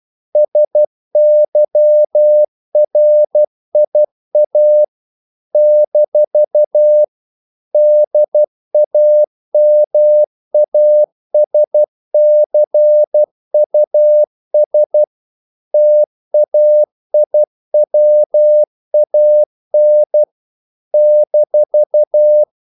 Morse code, heard as a keyed tone, about 12 wpm, 600 Hz. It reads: SYRIA - DAMASCUS TAIWAN -
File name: CW / Practice / NationalCapitals12